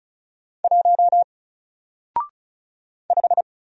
Morse code, heard as a keyed tone, about 34 wpm, 700 Hz. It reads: 1 5
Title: Single Letter-Number Alt 35wpm